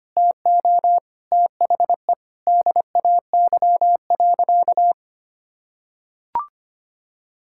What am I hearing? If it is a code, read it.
Morse code, heard as a keyed tone, about 25 wpm, 700 Hz. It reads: TO THE DAY.
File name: practice / 25wpm / Difficult Sentences with Hints - 25wpm